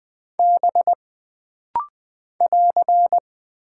Morse code, heard as a keyed tone, about 20 wpm, 700 Hz. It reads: B <AR>
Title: Single Character (Letter:Number:Punctuation:Prosign) - Rapid-Fire - 20wpm